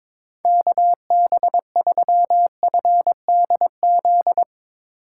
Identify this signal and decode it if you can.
Morse code, heard as a keyed tone, about 21 wpm, 700 Hz. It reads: KB3FDZ
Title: USA Call Signs - 6 Characters - 3x Character Spacing - 22wpm